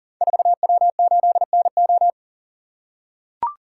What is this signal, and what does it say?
Morse code, heard as a keyed tone, about 40 wpm, 690 Hz. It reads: 4W8NO
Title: Code Groups - 5 Letter:Numbers - 40wpm